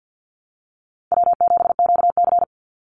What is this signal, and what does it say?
Morse code, heard as a keyed tone, about 50 wpm, 700 Hz. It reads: W7QC
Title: Easy Call Signs Repeated 3x - 50wpm